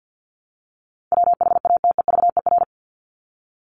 Morse code, heard as a keyed tone, about 50 wpm, 700 Hz. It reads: WHATEVER
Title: Top 1001-1250 Words 50wpm